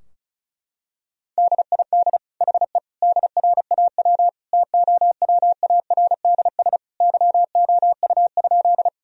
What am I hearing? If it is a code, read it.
Morse code, heard as a keyed tone, about 34 wpm, 700 Hz. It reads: DID HE DRAW TOWARDS YOU?
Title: Sentences from Top 1000 Words - Review Part 2 - 35wpm